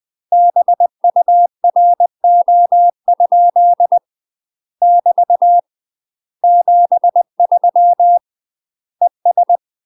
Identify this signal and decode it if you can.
Morse code, heard as a keyed tone, about 20 wpm, 700 Hz. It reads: BURO? <BT> 73 ES
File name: UK MASTER QSOs TRAINING / 20WPM / UK MASTER QSOs 0015